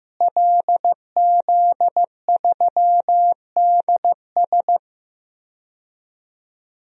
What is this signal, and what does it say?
Morse code, heard as a keyed tone, about 15 wpm, 700 Hz. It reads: LZ3DS
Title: DXCC-100 Call Signs - 5 Characters - Repeated 2x - 15wpm